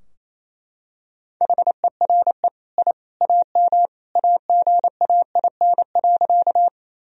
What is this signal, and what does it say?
Morse code, heard as a keyed tone, about 28 wpm, 700 Hz. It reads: HERE I AM AGAIN.
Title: Sentences from Top 1000 Words - Review Part 2 - 28wpm